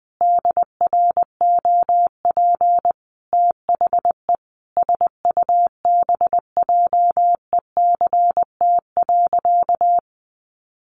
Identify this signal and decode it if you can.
Morse code, heard as a keyed tone, about 20 wpm, 700 Hz. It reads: DROP THE SUBJECT.
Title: Sentences from Top 1400 Words 20wpm